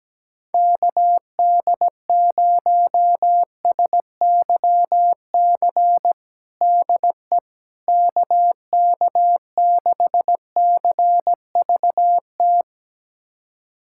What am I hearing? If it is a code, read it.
Morse code, heard as a keyed tone, about 17 wpm, 700 Hz. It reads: KD0SYC DE KK6CVT
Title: USA Callsign DE Callsign 17wpm